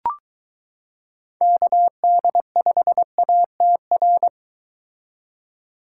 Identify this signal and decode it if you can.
Morse code, heard as a keyed tone, about 23 wpm, 700 Hz. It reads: KD5ATR